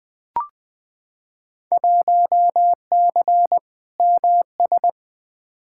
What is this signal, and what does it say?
Morse code, heard as a keyed tone, about 20 wpm, 700 Hz. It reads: 1C MS
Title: Field Day Exchanges Encore - 20wpm